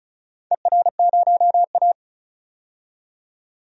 Morse code, heard as a keyed tone, about 36 wpm, 700 Hz. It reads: ER0A
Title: DXCC-100 Call Signs - 4 Characters - 35wpm